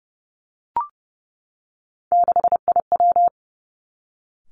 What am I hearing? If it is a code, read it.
Morse code, heard as a keyed tone, about 30 wpm, 700 Hz. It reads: 6IW
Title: Three Challenging Letter-Number Combinations 30wpm